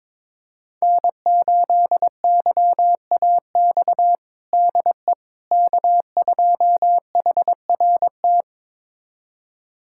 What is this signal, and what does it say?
Morse code, heard as a keyed tone, about 22 wpm, 700 Hz. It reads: N8YAX DE K2HRT
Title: USA Callsign DE Callsign - 5 Characters - 22wpm